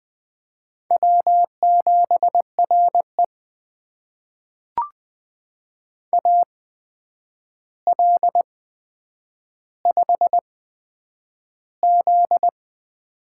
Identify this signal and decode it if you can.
Morse code, heard as a keyed tone, about 20 wpm, spaced as for 9 wpm, 700 Hz. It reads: W7RE AL5Z
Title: USA Call Signs - 4 Characters - 8x Character Spacing - 20wpm